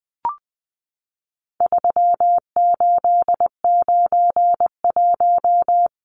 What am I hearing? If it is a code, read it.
Morse code, heard as a keyed tone, about 20 wpm, 700 Hz. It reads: 3891
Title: Numbers - 4-Digit 20wpm